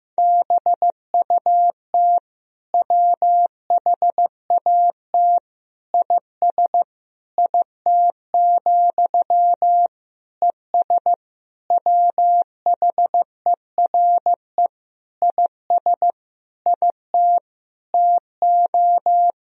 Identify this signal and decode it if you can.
Morse code, heard as a keyed tone, about 15 wpm, 700 Hz. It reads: BUT WHAT IS IT, ES WHERE IS IT TO
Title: Sentences from Top 300 Words and Abbreviations 15wpm